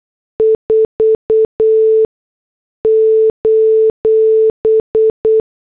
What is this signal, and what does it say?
Morse code, heard as a keyed tone, about 8 wpm, spaced as for 6 wpm, 440 Hz. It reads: 4: